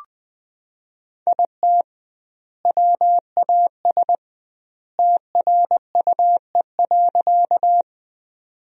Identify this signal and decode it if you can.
Morse code, heard as a keyed tone, about 20 wpm, 700 Hz. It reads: IT WAS TRUE.
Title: 2x Word-Spacing - Sentences from Top 400 Words - 20wpm